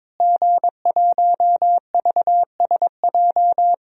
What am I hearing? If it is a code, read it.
Morse code, heard as a keyed tone, about 22 wpm, 700 Hz. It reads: G1VSJ
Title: DXCC-100 Call Signs - 5 Characters - Repeated 2x - 22wpm